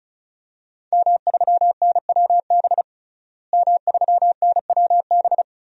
Morse code, heard as a keyed tone, about 34 wpm, 700 Hz. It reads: M3NWB M3NWB
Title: DXCC-100 Call Signs - 5 Characters - Repeated 3x - 35wpm